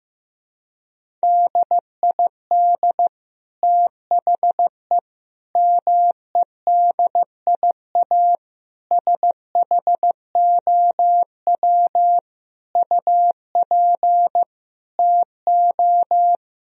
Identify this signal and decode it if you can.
Morse code, heard as a keyed tone, about 15 wpm, 700 Hz. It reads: DID THE MEDIA SHOW UP TO
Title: Sentences from Top 1000 Words - Review Part 4 - 15wpm